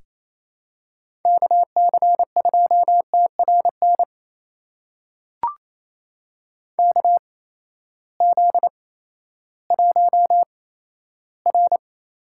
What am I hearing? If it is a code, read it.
Morse code, heard as a keyed tone, about 27 wpm, spaced as for 14 wpm, 700 Hz. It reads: KC2TRN KZ1R
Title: USA Call Signs - 8x Character Spacing - 28wpm